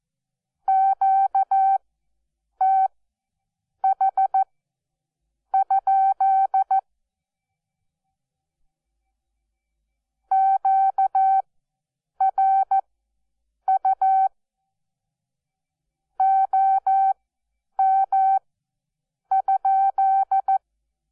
Morse code, heard as a keyed tone, about 14 wpm, spaced as for 7 wpm, 770 Hz. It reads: QTH? QRU OM?
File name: Onlinelehrgang / morsen / DJ4UF Morsen12